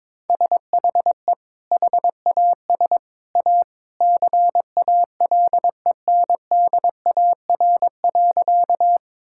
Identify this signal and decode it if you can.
Morse code, heard as a keyed tone, about 22 wpm, 700 Hz. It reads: SHE HAS A CALENDAR.
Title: Sentences from Top 1600 Words 22wpm